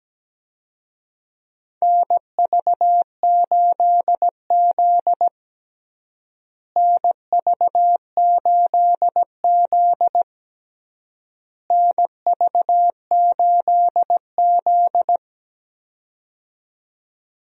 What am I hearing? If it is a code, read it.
Morse code, heard as a keyed tone, about 17 wpm, 700 Hz. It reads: NV8Z NV8Z NV8Z
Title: USA Call Signs - 4 Characters - Repeated 3x - 17wpm